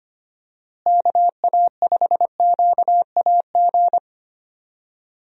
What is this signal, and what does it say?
Morse code, heard as a keyed tone, about 25 wpm, 700 Hz. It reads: KA5QAG
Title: USA Call Signs - 6 Characters - Repeated 3x - 25wpm